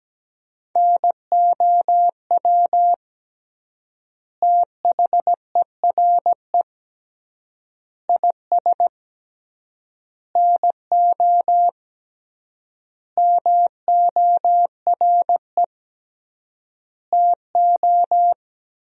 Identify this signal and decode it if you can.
Morse code, heard as a keyed tone, about 17 wpm, 700 Hz. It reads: NOW THERE IS NO MORE TO
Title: 3x Word-Spacing - Sentences from Top 100 Words - 17wpm